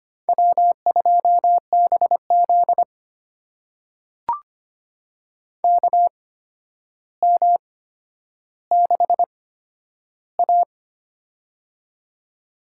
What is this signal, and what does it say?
Morse code, heard as a keyed tone, about 25 wpm, spaced as for 11 wpm, 700 Hz. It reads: W2BZ KM6A
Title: USA Call Signs - 4 Characters - 8x Character Spacing - 25wpm